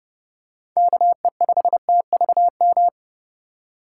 Morse code, heard as a keyed tone, about 30 wpm, 700 Hz. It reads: KE5TVM
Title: USA Call Signs - 6 Characters - 3x Character Spacing - 30wpm